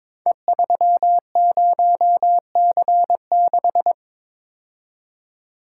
Morse code, heard as a keyed tone, about 22 wpm, 700 Hz. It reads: E30C6
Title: Code Groups - 5 Letter:Numbers - 22wpm